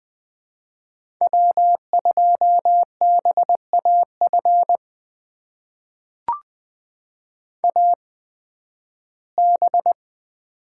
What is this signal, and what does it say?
Morse code, heard as a keyed tone, about 20 wpm, spaced as for 13 wpm, 700 Hz. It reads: W2BAF AB
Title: USA Call Signs - 5 Characters - 8x Character Spacing - 20wpm